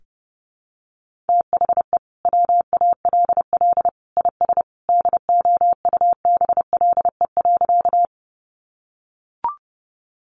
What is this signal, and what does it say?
Morse code, heard as a keyed tone, about 30 wpm, 700 Hz. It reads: THE WALL IS DOUBLE.
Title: Sentences from Top 1400 Words 30wpm